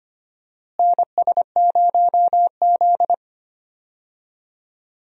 Morse code, heard as a keyed tone, about 25 wpm, 700 Hz. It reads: NS0Z
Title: USA Call Signs - 4 Characters - Repeated 3x - 25wpm